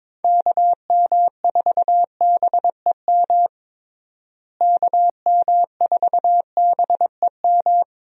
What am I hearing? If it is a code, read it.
Morse code, heard as a keyed tone, about 22 wpm, 700 Hz. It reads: KM4BEM KM4BEM
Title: USA Call Signs - 6 Characters - Repeated 2x - 22wpm